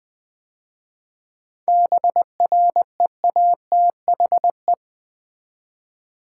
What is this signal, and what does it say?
Morse code, heard as a keyed tone, about 20 wpm, 700 Hz. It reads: BREATHE
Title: 7 Letter Words - First 3 Letters Given - 20wpm